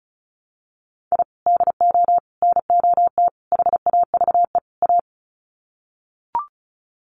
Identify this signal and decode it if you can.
Morse code, heard as a keyed tone, about 34 wpm, 700 Hz. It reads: I DO NOT HAVE A
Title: Sets of 5 Words 35wpm